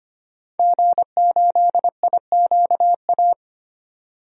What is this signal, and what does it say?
Morse code, heard as a keyed tone, about 25 wpm, 700 Hz. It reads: G8IQA
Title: DXCC-100 Call Signs - 5 Characters - 25wpm